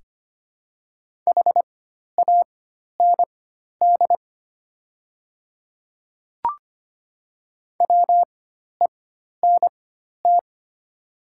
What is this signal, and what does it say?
Morse code, heard as a keyed tone, about 25 wpm, spaced as for 12 wpm, 700 Hz. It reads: HAND WENT
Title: 4x Character Spacing - Top 300 Words - 25wpm